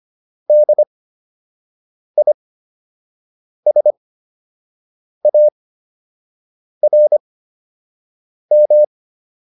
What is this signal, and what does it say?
Morse code, heard as a keyed tone, about 25 wpm, spaced as for 6 wpm, 600 Hz. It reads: DISARM